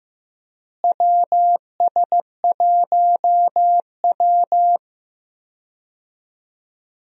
Morse code, heard as a keyed tone, about 15 wpm, 700 Hz. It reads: WS1W
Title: USA Call Signs - 4 Characters - Repeated 2x - 15wpm